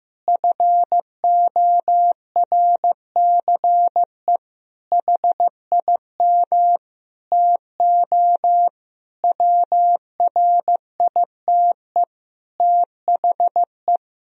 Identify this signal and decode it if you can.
Morse code, heard as a keyed tone, about 15 wpm, 700 Hz. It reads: FORCE HIM TO WRITE THE
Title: Sentences from Top 1000 Words - Review Part 3 - 15wpm